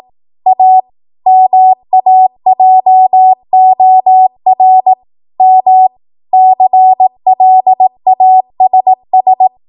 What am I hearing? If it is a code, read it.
Morse code, heard as a keyed tone, about 18 wpm, 750 Hz. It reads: A MAJOR M CLASS
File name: Morse / 18WPM